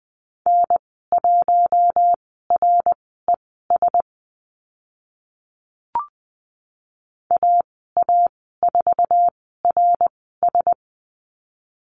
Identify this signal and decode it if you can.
Morse code, heard as a keyed tone, about 20 wpm, spaced as for 14 wpm, 700 Hz. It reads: N1RES AA4RS
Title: 082 Lesson 14g - Number 2 - Call Signs 20wpm